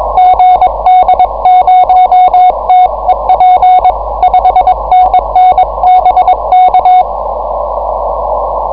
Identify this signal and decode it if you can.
Morse code, heard as a keyed tone, about 22 wpm, 740 Hz. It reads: GD0TEP 5NN BK
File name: 6M audio / 9g5an cw cycle22